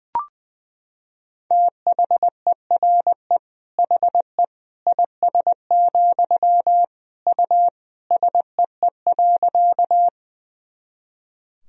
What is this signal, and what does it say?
Morse code, heard as a keyed tone, about 20 wpm, 700 Hz. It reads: THERE HE IS, U SEE.
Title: Sentences from Top 100 Words and Abbreviations 20wpm